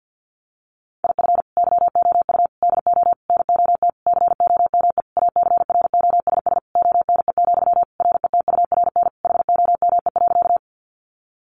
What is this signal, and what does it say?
Morse code, heard as a keyed tone, about 50 wpm, 700 Hz. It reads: IF YOU DO NOT COME ACROSS ONE, RETURN HOME.
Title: Sentences from Top 800 Words 50wpm